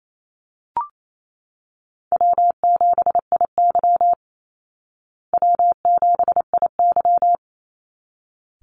Morse code, heard as a keyed tone, about 28 wpm, 700 Hz. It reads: W7IY W7IY
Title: Easy Call Signs Repeated 2x - 28wpm